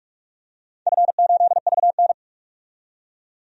Morse code, heard as a keyed tone, about 41 wpm, 700 Hz. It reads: F8VN
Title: DXCC-100 Call Signs - 4 Characters - 45wpm